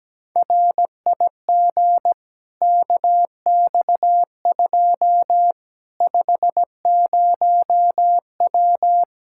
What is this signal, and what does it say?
Morse code, heard as a keyed tone, about 17 wpm, 700 Hz. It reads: RIG KX2 50W
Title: Practice QSO - Exchange 2 - Info - 17wpm